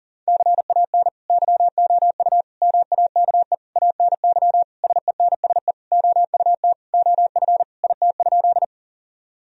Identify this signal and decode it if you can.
Morse code, heard as a keyed tone, about 40 wpm, 700 Hz. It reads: CAN YOU MAKE ANY SENSE OUT OF IT?
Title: Sentences from Top 1000 Words - Review Part 4 - 40wpm